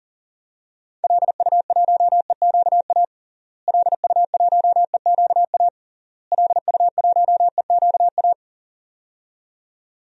Morse code, heard as a keyed tone, about 40 wpm, 700 Hz. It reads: LU1EQA LU1EQA LU1EQA
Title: DXCC-100 Call Signs - 6 Characters - Repeated 3x - 40wpm